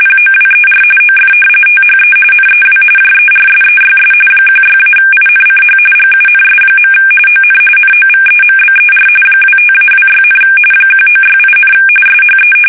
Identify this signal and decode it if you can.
RTTY, about 45.45 baud. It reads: POWER OF THE NEW JERSEYS 16-INCH GUNS. ON NOVEMBER 25-26, 1968, SHE
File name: fsk850 / battleshipnj